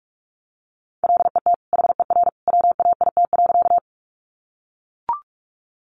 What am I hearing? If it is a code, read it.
Morse code, heard as a keyed tone, about 44 wpm, 700 Hz. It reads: LET HER WAIT.
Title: Sentences from Top 500 Words 45wpm